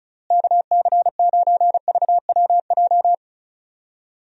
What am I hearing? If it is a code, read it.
Morse code, heard as a keyed tone, about 34 wpm, 700 Hz. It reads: KC9VWJ
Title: USA Call Signs - 6 Characters - Repeated 3x - 35wpm